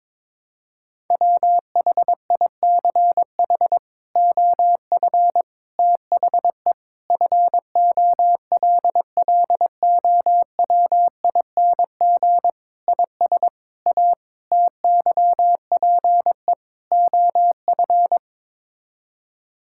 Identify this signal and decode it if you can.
Morse code, heard as a keyed tone, about 22 wpm, 700 Hz. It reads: WHICH OF THE FOLLOWING IS A TYPE OF
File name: practice / 22wpm / Sets of 8 Words - 22wpm